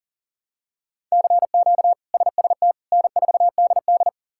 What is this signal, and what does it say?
Morse code, heard as a keyed tone, about 37 wpm, 700 Hz. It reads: CQ SST N4DD
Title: K1USN SST QSO Practice - 40wpm